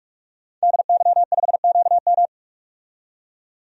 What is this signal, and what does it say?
Morse code, heard as a keyed tone, about 46 wpm, 700 Hz. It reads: DY5QM
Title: DXCC-100 Call Signs - 5 Characters - 45wpm